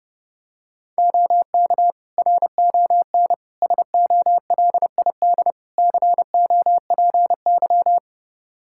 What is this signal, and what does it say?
Morse code, heard as a keyed tone, about 30 wpm, 700 Hz. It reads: OK RON SOLID COPY
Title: Practice QSO - Exchange 3 - More Info - 30wpm